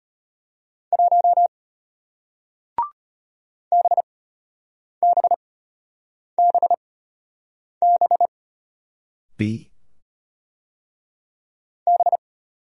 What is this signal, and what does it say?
Morse code, heard as a keyed tone, about 33 wpm, spaced as for 8 wpm, 700 Hz. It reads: 1 BBBB B